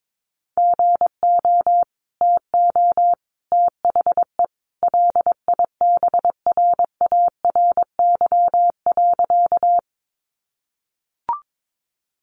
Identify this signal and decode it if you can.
Morse code, heard as a keyed tone, about 22 wpm, 700 Hz. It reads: GO TO THE LIBRARY.
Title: Sentences from Top 600 Words 22wpm